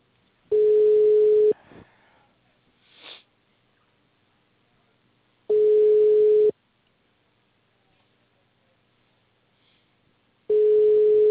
Ringback tone (Europe, 425 Hz).
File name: out- -2046- - - .32415